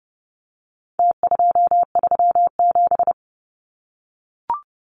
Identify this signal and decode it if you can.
Morse code, heard as a keyed tone, about 30 wpm, 700 Hz. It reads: T237
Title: Numbers - Contest Style - 4-Digit 30wpm